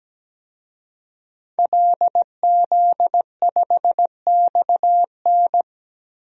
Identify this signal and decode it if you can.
Morse code, heard as a keyed tone, about 17 wpm, 700 Hz. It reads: LZ5XN
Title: DXCC-100 Call Signs - 5 Characters - 17wpm